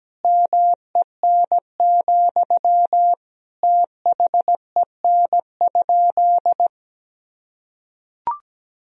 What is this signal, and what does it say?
Morse code, heard as a keyed tone, about 17 wpm, 700 Hz. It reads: MEN, THEN?
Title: Sentences from Top 1000 Words - Review Part 1 - 17wpm